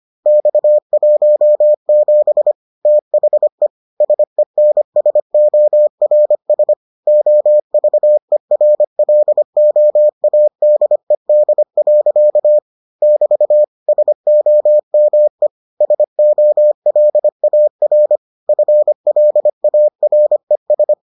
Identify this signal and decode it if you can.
Morse code, heard as a keyed tone, about 25 wpm, 600 Hz. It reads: X17 THE SENSORS OVERLOADED. <BT> SOME SOLAR FLARES